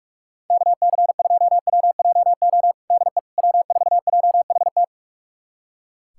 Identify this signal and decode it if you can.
Morse code, heard as a keyed tone, about 42 wpm, 700 Hz. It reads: KC2WJO DE W4JHT
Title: USA Callsign DE Callsign 45wpm